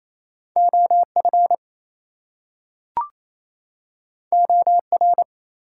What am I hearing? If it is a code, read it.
Morse code, heard as a keyed tone, about 28 wpm, 700 Hz. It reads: OF OR
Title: Two Letter Words - 28wpm